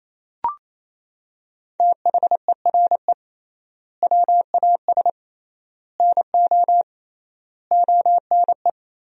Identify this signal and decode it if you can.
Morse code, heard as a keyed tone, about 28 wpm, 700 Hz. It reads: THERE WAS NO ONE
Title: 3x Word-Spacing - Sentences from Top 200 Words - 28wpm